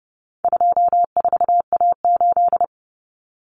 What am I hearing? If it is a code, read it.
Morse code, heard as a keyed tone, about 28 wpm, 700 Hz. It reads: 24A8
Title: Common Cut Numbers - 4-Digit 30wpm